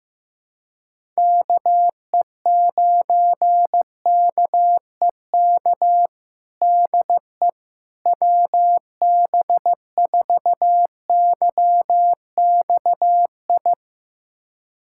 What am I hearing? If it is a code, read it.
Morse code, heard as a keyed tone, about 15 wpm, 700 Hz. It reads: KE9KEK DE WB4YXI
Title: USA Callsign DE Callsign - 6 Characters - 15wpm